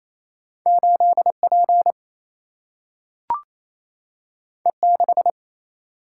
Morse code, heard as a keyed tone, about 28 wpm, 700 Hz. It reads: 8 P E 6
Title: Single-Letter Single-Number 28wpm